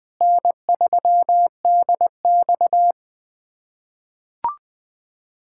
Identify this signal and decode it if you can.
Morse code, heard as a keyed tone, about 20 wpm, 700 Hz. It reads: N3DX
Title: USA Callsign DE Callsign - 4 Characters - 20wpm